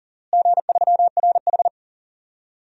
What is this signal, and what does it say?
Morse code, heard as a keyed tone, about 40 wpm, 700 Hz. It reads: G3RH
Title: DXCC-100 Call Signs - 4 Characters - Repeated 3x - 40wpm